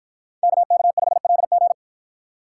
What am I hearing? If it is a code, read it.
Morse code, heard as a keyed tone, about 53 wpm, 700 Hz. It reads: KK5LG